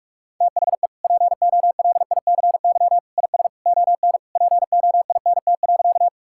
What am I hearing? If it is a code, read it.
Morse code, heard as a keyed tone, about 41 wpm, 690 Hz. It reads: THE POLICY IS ON POINT.
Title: Sentences from Top 1000 Words - Review Part 5 - 45wpm